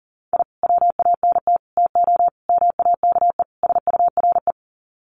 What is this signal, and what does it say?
Morse code, heard as a keyed tone, about 37 wpm, 700 Hz. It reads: I WANT TO MAKE SURE
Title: Sets of 5 Words Encore2 40wpm